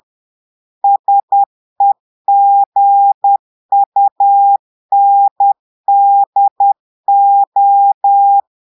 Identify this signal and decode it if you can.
Morse code, heard as a keyed tone, about 10 wpm, 800 Hz. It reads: SEGUNDO
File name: Portugais - texte en clair 1 - vitesse - 10 mots minute